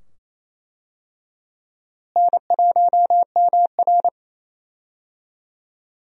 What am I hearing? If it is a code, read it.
Morse code, heard as a keyed tone, about 28 wpm, 700 Hz. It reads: N1MR
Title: USA Call Signs - 4 Characters - Repeated 2x - 28wpm